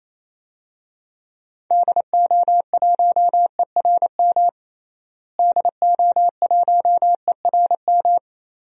Morse code, heard as a keyed tone, about 28 wpm, 700 Hz. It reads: DO1ERM DO1ERM
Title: DXCC-100 Call Signs - 6 Characters - Repeated 3x - 28wpm